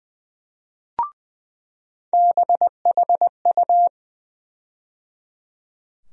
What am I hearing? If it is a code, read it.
Morse code, heard as a keyed tone, about 20 wpm, 700 Hz. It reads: BHU
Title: USA Call Sign Postfixes - 3 Letters 20wpm